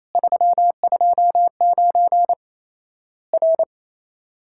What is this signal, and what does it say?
Morse code, heard as a keyed tone, about 28 wpm, 700 Hz. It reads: 329 R
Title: Worked All Europe DX Contest - QTC Exchange - 28wpm